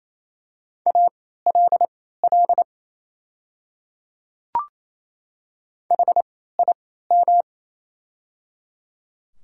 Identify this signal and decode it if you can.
Morse code, heard as a keyed tone, about 27 wpm, spaced as for 16 wpm, 700 Hz. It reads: ALL HIM
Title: 3x Character Spacing - Top 100 Words - 28wpm